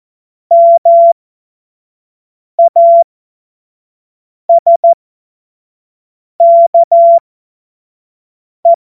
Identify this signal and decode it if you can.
Morse code, heard as a keyed tone, about 14 wpm, spaced as for 5 wpm, 680 Hz. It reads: MASKE
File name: Morsekurs / Witze / Witz 31. August 2021, 25 F70